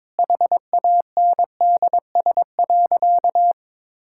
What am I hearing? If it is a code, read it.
Morse code, heard as a keyed tone, about 22 wpm, 700 Hz. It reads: HANDS.
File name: Sentences with 7-Word Phrases - 22wpm